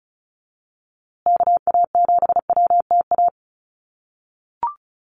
Morse code, heard as a keyed tone, about 34 wpm, 700 Hz. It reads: KA7WTA
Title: USA Call Signs Repeated 3x 35wpm